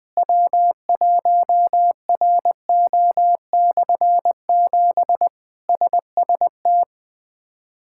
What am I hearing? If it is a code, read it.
Morse code, heard as a keyed tone, about 21 wpm, 700 Hz. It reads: W1RO/7 SST
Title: K1USN SST QSO Practice - 20wpm